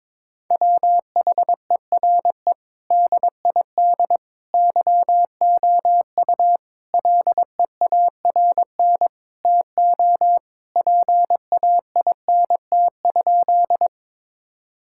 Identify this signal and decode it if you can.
Morse code, heard as a keyed tone, about 22 wpm, 700 Hz. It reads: WHERE DID YOU LEARN TO PAINT?
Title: Sentences from Top 1000 Words - Review Part 6 - 22wpm